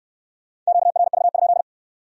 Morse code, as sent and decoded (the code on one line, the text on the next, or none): --.- .-. ...- ..--..
QRV?